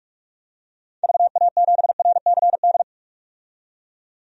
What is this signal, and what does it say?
Morse code, heard as a keyed tone, about 45 wpm, 700 Hz. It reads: VA7RCD
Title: DXCC-100 Call Signs - 6 Characters - 45wpm